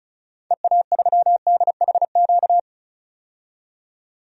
Morse code, heard as a keyed tone, about 36 wpm, 700 Hz. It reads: EA3DHQ
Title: DXCC-100 Call Signs - 6 Characters - 35wpm